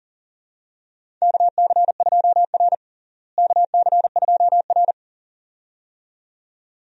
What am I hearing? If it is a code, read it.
Morse code, heard as a keyed tone, about 40 wpm, 700 Hz. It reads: KC2R KC2R
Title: USA Call Signs - 4 Characters - Repeated 2x - 40wpm